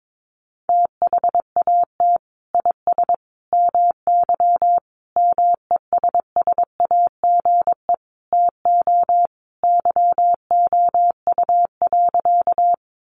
Morse code, heard as a keyed tone, about 22 wpm, 700 Hz. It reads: THAT IS MY MESSAGE TO YOU.
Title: Sentences from Top 1100 Words 22wpm